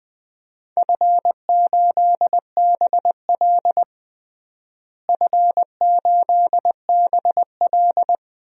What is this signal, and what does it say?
Morse code, heard as a keyed tone, about 21 wpm, 700 Hz. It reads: F8BL F8BL
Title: DXCC-100 Call Signs - 4 Characters - Repeated 2x - 20wpm